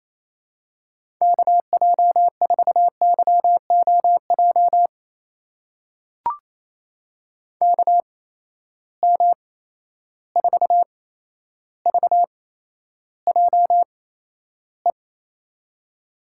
Morse code, heard as a keyed tone, about 28 wpm, spaced as for 12 wpm, 700 Hz. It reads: KJ4YOJ KM4VJE
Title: USA Call Signs - 6 Characters - 8x Character Spacing - 28wpm